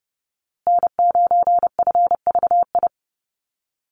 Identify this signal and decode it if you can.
Morse code, heard as a keyed tone, about 30 wpm, 700 Hz. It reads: N9FVI
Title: USA Call Signs Repeated 3x 30wpm